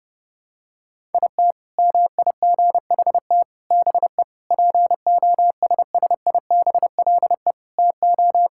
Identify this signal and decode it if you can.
Morse code, heard as a keyed tone, about 32 wpm, 700 Hz. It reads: IT MIGHT BE POSSIBLE TO
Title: Sentences from Top 1000 Words - Review Part 6 - 30wpm